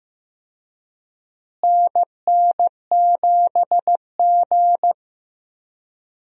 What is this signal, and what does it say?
Morse code, heard as a keyed tone, about 15 wpm, 700 Hz. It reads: NN7G
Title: USA Call Signs - 4 Characters - Repeated 2x - 15wpm